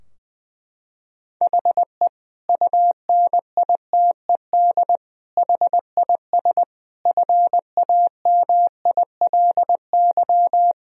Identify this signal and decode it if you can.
Morse code, heard as a keyed tone, about 20 wpm, 700 Hz. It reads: HE UNITED HIS FAMILY
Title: Sentences from Top 1000 Words - Review Part 4 - 20wpm